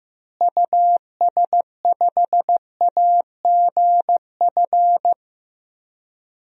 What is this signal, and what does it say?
Morse code, heard as a keyed tone, about 15 wpm, 700 Hz. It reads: US5AGF
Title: DXCC-100 Call Signs - 6 Characters - Repeated 2x - 15wpm